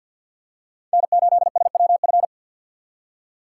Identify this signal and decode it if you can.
Morse code, heard as a keyed tone, about 47 wpm, 700 Hz. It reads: N8SWF
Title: USA Call Signs - 5 Characters - 6x Character Spacing - 50wpm